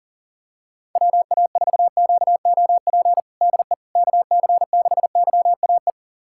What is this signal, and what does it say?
Morse code, heard as a keyed tone, about 40 wpm, 700 Hz. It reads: WA4QOP DE KC6YAE
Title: USA Callsign DE Callsign 40wpm